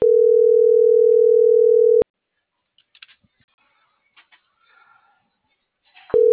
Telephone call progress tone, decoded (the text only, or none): ringback tone